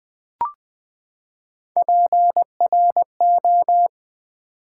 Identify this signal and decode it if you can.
Morse code, heard as a keyed tone, about 20 wpm, 700 Hz. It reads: PRO
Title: Build A Word - 3 Letters At A Time - 8-Letter Words - 20wpm